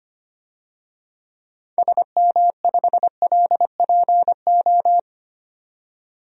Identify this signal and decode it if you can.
Morse code, heard as a keyed tone, about 25 wpm, 690 Hz. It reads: SM5LPO
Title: DXCC-100 Call Signs - 6 Characters - 25wpm